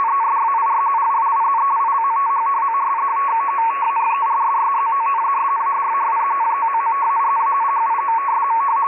Two-tone teleprinter signal, about 200 Hz shift.